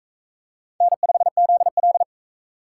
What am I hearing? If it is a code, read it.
Morse code, heard as a keyed tone, about 43 wpm, 700 Hz. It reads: N5ZL